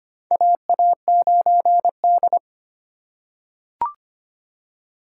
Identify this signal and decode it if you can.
Morse code, heard as a keyed tone, about 25 wpm, 700 Hz. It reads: AA9D
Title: USA Call Signs - 4 Characters - 3x Character Spacing - 25wpm